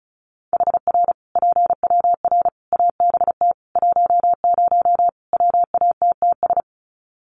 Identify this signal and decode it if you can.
Morse code, heard as a keyed tone, about 34 wpm, 700 Hz. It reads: HR PWR ABT 10 WATTS
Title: Common QSO Phrases 35wpm